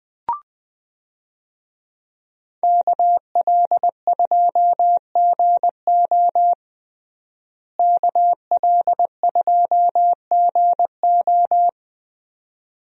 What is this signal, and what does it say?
Morse code, heard as a keyed tone, about 21 wpm, 700 Hz. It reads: KL2GO KL2GO
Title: USA Call Signs - 5 Characters - Repeated 2x - 20wpm